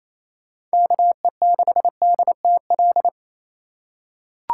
Morse code, heard as a keyed tone, about 28 wpm, 700 Hz. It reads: KE6DTL
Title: USA Call Signs - 4x Character Spacing - 28wpm